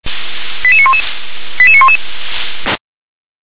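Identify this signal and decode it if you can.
Five-tone selective call (ZVEI), heard as quick runs of five tones, tones of about 70 ms. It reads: 88911 88911
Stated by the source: Five-tone ZVEI selective call received on an SDR.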